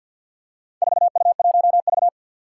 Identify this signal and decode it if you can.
Morse code, heard as a keyed tone, about 52 wpm, 700 Hz. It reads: 4U1V
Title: Four Challenging Letter-Number Combinations 50wpm